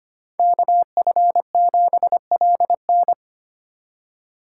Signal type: Morse code

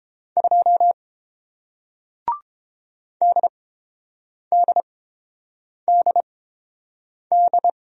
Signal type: Morse code

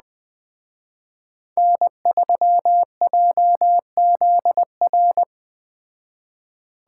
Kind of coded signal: Morse code

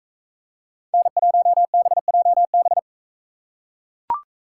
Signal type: Morse code

